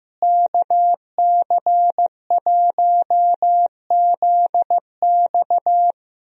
Morse code, heard as a keyed tone, about 15 wpm, 700 Hz. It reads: KC1ZX
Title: USA Call Signs - 5 Characters - Repeated 2x - 15wpm